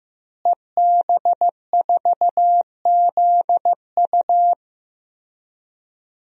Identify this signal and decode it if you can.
Morse code, heard as a keyed tone, about 15 wpm, 700 Hz. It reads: EB4ZU
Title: DXCC-100 Call Signs - 5 Characters - Repeated 2x - 15wpm